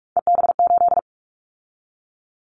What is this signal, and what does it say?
Morse code, heard as a keyed tone, about 44 wpm, 700 Hz. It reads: EB8
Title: Callsign Prefixes for Top 100 DXCC - 45wpm